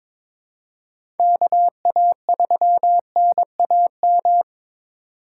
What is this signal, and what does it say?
Morse code, heard as a keyed tone, about 22 wpm, 700 Hz. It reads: KA3NAM
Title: K1USN SST QSO Practice - 22wpm